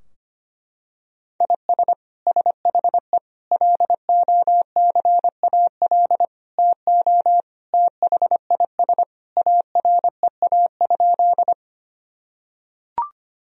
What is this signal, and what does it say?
Morse code, heard as a keyed tone, about 25 wpm, 700 Hz. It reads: IS SHE LOCAL TO THIS AREA?
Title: Sentences from Top 1000 Words - Review Part 3 - 25wpm